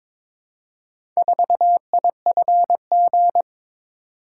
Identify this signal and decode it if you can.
Morse code, heard as a keyed tone, about 23 wpm, 700 Hz. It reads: 4IFG
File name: practice / 22wpm / Four Letter-Number Combinations - 22wpm